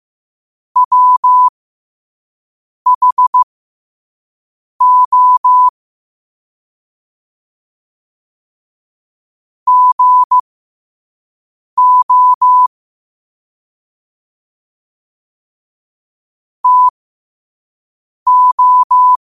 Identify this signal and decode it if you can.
Morse code, heard as a keyed tone, about 15 wpm, spaced as for 5 wpm, 1000 Hz. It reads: WHO GO TO
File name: qotdfiles / 5wpm